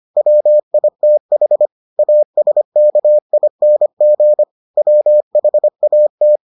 Morse code, heard as a keyed tone, about 25 wpm, 600 Hz. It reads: WITH ASKING WHAT